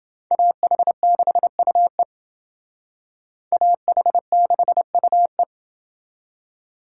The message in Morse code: AH6UE AH6UE

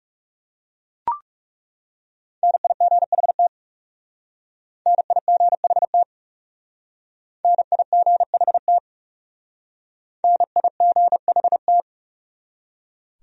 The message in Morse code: NIGHT NIGHT NIGHT NIGHT

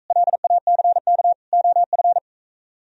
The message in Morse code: LACK OF

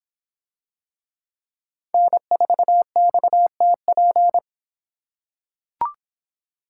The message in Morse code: N4XTP

